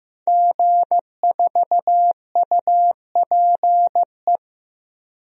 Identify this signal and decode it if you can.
Morse code, heard as a keyed tone, about 16 wpm, 700 Hz. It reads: G4UPE
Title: DXCC-100 Call Signs - 5 Characters - 15wpm